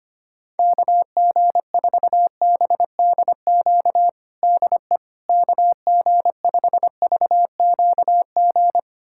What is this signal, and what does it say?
Morse code, heard as a keyed tone, about 25 wpm, 700 Hz. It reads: KG4BDQ DE KG5VQG
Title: USA Callsign DE Callsign 25wpm